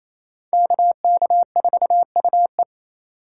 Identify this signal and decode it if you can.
Morse code, heard as a keyed tone, about 34 wpm, 700 Hz. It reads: KK4UE
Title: Speed Racing - USA 2x2 Call Signs 28wpm